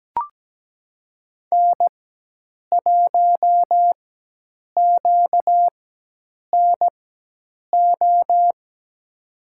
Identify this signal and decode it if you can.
Morse code, heard as a keyed tone, about 17 wpm, spaced as for 8 wpm, 700 Hz. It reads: N1QNO
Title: USA Call Signs - 4x Character Spacing - 17wpm